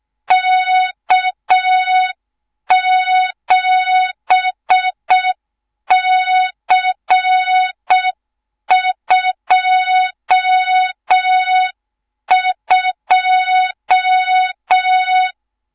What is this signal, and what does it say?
Morse code, heard as a keyed tone, about 6 wpm, 3700 Hz. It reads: K7C22
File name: Morze-HA2MN / hang / 24 GYAKORLAT